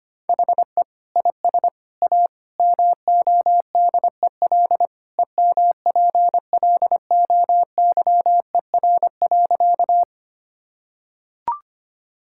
Morse code, heard as a keyed tone, about 25 wpm, 700 Hz. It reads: HE IS A MODEL EMPLOYER.
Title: Sentences from Top 1600 Words 25wpm